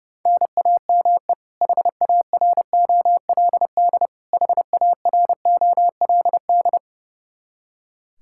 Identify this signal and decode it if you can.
Morse code, heard as a keyed tone, about 30 wpm, 700 Hz. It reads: NAME HAROLD HAROLD
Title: K1USN SST QSO Practice 30wpm